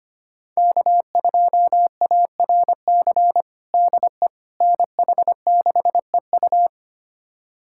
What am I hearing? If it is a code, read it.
Morse code, heard as a keyed tone, about 25 wpm, 700 Hz. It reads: K2ARC DE NH6EU
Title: USA Callsign DE Callsign - 5 Characters - 25wpm